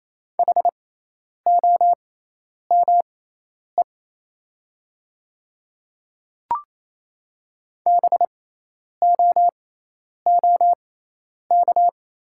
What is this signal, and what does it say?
Morse code, heard as a keyed tone, about 28 wpm, spaced as for 10 wpm, 700 Hz. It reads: HOME BOOK
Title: 6x Character Spacing - Top 300 Words - 28wpm